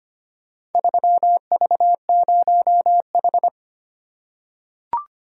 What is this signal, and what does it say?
Morse code, heard as a keyed tone, about 25 wpm, 700 Hz. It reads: 3V0H
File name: practice / 25wpm / Four Letter-Number Combinations - 25wpm